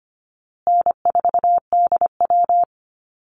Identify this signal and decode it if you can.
Morse code, heard as a keyed tone, about 25 wpm, 700 Hz. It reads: N4DW
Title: Easy Call Signs Repeated 3x - 25wpm